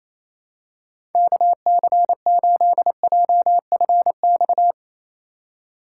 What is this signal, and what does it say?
Morse code, heard as a keyed tone, about 28 wpm, 700 Hz. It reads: KC8JFX
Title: USA Call Signs - 6 Characters - 4x Character Spacing - 28wpm